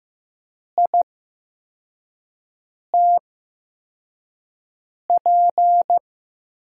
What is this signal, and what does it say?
Morse code, heard as a keyed tone, about 15 wpm, spaced as for 4 wpm, 700 Hz. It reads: ITP
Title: USA Call Signs - 8x Character Spacing - 15wpm